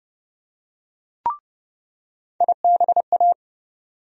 Morse code, heard as a keyed tone, about 32 wpm, 700 Hz. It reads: IBA
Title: USA Call Sign Postfixes - 3 Letters 30wpm